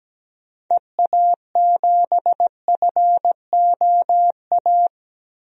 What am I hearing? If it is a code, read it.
Morse code, heard as a keyed tone, about 17 wpm, 700 Hz. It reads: EA7FOA
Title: DXCC-100 Call Signs - 6 Characters - 17wpm